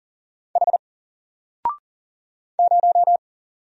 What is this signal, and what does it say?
Morse code, heard as a keyed tone, about 40 wpm, 700 Hz. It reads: H 0
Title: Single Letter-Number Alt 40wpm